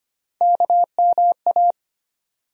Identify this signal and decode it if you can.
Morse code, heard as a keyed tone, about 25 wpm, 700 Hz. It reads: KMA
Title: USA Call Sign Postfixes - 3 Letters 25wpm